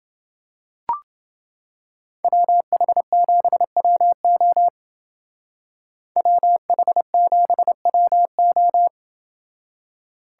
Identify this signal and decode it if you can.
Morse code, heard as a keyed tone, about 28 wpm, 700 Hz. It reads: WH7WO WH7WO